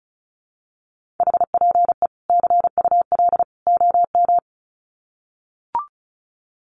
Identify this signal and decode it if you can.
Morse code, heard as a keyed tone, about 34 wpm, 700 Hz. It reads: HPE CUL OM
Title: Common QSO Phrases 35wpm